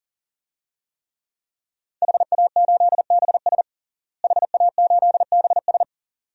Morse code, heard as a keyed tone, about 40 wpm, 700 Hz. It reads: HA8BS HA8BS
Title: DXCC-100 Call Signs - 5 Characters - Repeated 2x - 40wpm